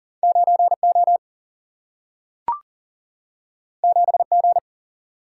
Morse code, heard as a keyed tone, about 42 wpm, 700 Hz. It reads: 9 O 7 G
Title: Single-Letter Single-Number 40wpm